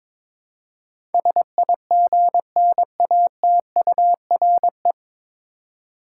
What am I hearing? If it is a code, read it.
Morse code, heard as a keyed tone, about 22 wpm, 700 Hz. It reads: SIGNATURE
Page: Morse Code Ninja practice 22wpm